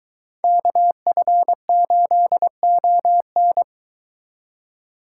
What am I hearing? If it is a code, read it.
Morse code, heard as a keyed tone, about 22 wpm, 700 Hz. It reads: KF8ON